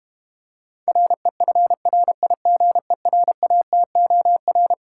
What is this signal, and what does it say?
Morse code, heard as a keyed tone, about 32 wpm, 700 Hz. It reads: REFRIGERATOR